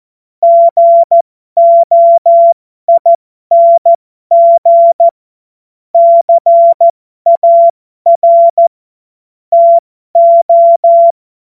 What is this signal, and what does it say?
Morse code, heard as a keyed tone, about 14 wpm, spaced as for 12 wpm, 680 Hz. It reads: GOING CAR TO
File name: Morsekurs / Witze